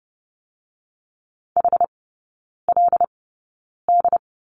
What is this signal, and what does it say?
Morse code, heard as a keyed tone, about 30 wpm, spaced as for 9 wpm, 700 Hz. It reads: HLD
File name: Lesson 16e - Letter c - Three Characters 30wpm